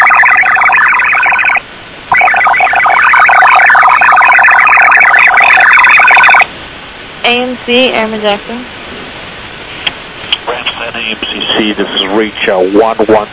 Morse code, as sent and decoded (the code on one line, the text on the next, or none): none